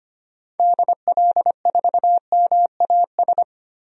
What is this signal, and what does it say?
Morse code, heard as a keyed tone, about 25 wpm, 700 Hz. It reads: DL4MAS
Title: DXCC-100 Call Signs - 6 Characters - Repeated 3x - 25wpm